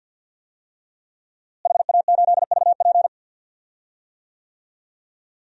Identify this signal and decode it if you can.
Morse code, heard as a keyed tone, about 52 wpm, 700 Hz. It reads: HA8VP
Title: DXCC-100 Call Signs - 5 Characters - Repeated 3x - 50wpm